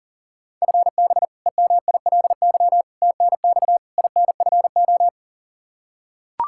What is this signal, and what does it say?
Morse code, heard as a keyed tone, about 40 wpm, 700 Hz. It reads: FB EMILY TNX INFO
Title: Practice QSO - Exchange 3 - More Info - 40wpm